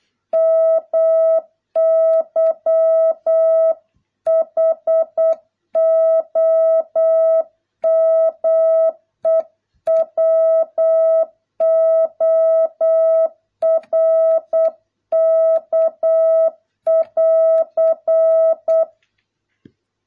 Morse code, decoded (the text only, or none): MY HOMEWORK <AR>